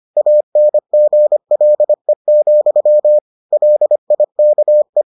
Morse code, heard as a keyed tone, about 25 wpm, 600 Hz. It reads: ANGLE, LIKE